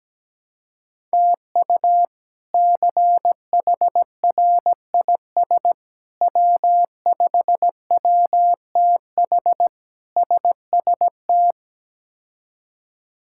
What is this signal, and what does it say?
Morse code, heard as a keyed tone, about 17 wpm, 700 Hz. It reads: TU CHRIS W5WTH SST
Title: K1USN SST QSO Practice - 17wpm